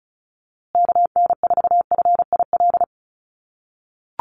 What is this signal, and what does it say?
Morse code, heard as a keyed tone, about 34 wpm, 700 Hz. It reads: KN4FIL
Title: USA Call Signs Repeated 3x 35wpm